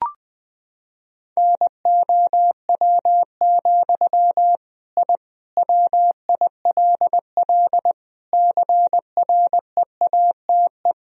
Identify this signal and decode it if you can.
Morse code, heard as a keyed tone, about 21 wpm, 700 Hz. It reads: NOW, I WILL CREATE
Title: Sentences from Top 1000 Words - Review Part 3 - 20wpm